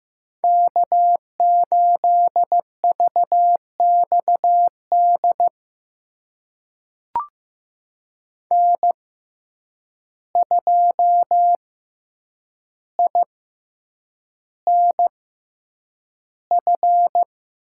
Morse code, heard as a keyed tone, about 15 wpm, spaced as for 8 wpm, 700 Hz. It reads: K8VXD N2INF
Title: USA Call Signs - 5 Characters - 6x Character Spacing - 15wpm